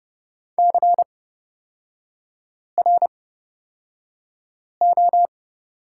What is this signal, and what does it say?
Morse code, heard as a keyed tone, about 30 wpm, spaced as for 5 wpm, 700 Hz. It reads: CRO
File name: Lesson 36e - BT - Three Characters 30wpm